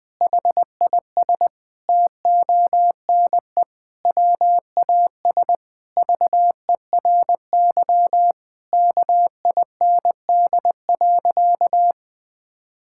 Morse code, heard as a keyed tone, about 21 wpm, 700 Hz. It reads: HIS TONE WAS VERY KIND.
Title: Sentences from Top 1600 Words 20wpm